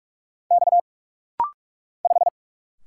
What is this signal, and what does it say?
Morse code, heard as a keyed tone, about 44 wpm, 690 Hz. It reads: X 5
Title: Single Letter-Number - Rapid-Fire 45wpm